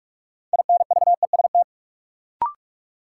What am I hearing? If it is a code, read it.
Morse code, heard as a keyed tone, about 42 wpm, 700 Hz. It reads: INVEST
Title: Sentences from Top 1100 Words - New Word Drill Alt - 45wpm